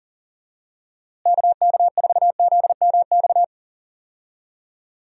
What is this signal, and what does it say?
Morse code, heard as a keyed tone, about 39 wpm, 700 Hz. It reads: KK4ZMX
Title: USA Call Signs - 6 Characters - Repeated 2x - 40wpm